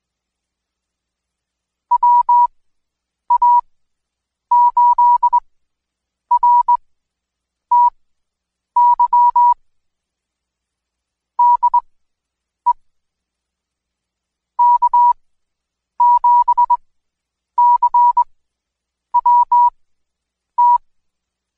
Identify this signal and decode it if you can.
Morse code, heard as a keyed tone, about 21 wpm, spaced as for 8 wpm, 970 Hz. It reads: WA8RTY DE K7CWT
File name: Code Quick / AudioDownld2 / Seg114-Day19-DayDrill 19